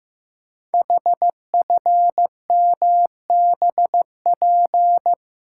Morse code, heard as a keyed tone, about 15 wpm, 700 Hz. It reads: HFMBP
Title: Code Groups - 5 Letter:Numbers - 15wpm